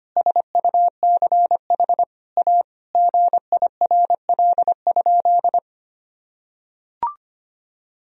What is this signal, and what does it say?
Morse code, heard as a keyed tone, about 25 wpm, 700 Hz. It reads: SUCH A GIRL?
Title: Sentences from Top 1000 Words - Review Part 1 - 25wpm